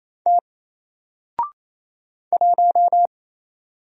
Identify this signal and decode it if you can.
Morse code, heard as a keyed tone, about 28 wpm, 700 Hz. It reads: T 1
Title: Single Character (Letter:Number:Punctuation:Prosign) Alt 28wpm